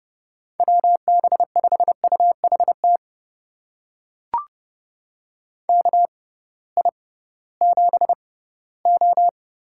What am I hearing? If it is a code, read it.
Morse code, heard as a keyed tone, about 30 wpm, spaced as for 17 wpm, 700 Hz. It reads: WB5UHT KI7O
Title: USA Call Signs - 6x Character Spacing - 30wpm